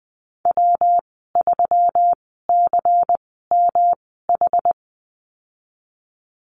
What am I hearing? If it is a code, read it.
Morse code, heard as a keyed tone, about 20 wpm, spaced as for 15 wpm, 700 Hz. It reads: W3CMH
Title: 128 Lesson 21g - Number 6 - Call Signs 20wpm